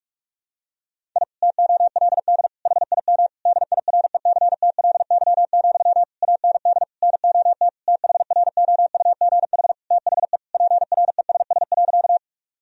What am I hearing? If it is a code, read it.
Morse code, heard as a keyed tone, about 46 wpm, 700 Hz. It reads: I TOLD HIM DIRECTLY, AND NOT THROUGH THE PRESS.
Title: Sentences from Top 1000 Words - Review Part 1 - 45wpm